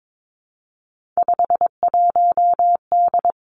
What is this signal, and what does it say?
Morse code, heard as a keyed tone, about 22 wpm, 700 Hz. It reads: 51D
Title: Three Challenging Letter-Number Combinations 22wpm